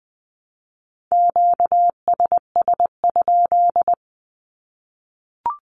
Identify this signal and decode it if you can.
Morse code, heard as a keyed tone, about 20 wpm, 700 Hz. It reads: QSS?
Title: Q-Codes Repeated 3x 20wpm